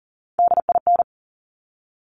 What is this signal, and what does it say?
Morse code, heard as a keyed tone, about 41 wpm, 690 Hz. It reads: DIN
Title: Top Three Letter Combinations in Words 40wpm